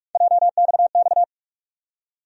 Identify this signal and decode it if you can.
Morse code, heard as a keyed tone, about 42 wpm, 700 Hz. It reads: JXX
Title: USA Call Sign Postfixes - 3 Letters 45wpm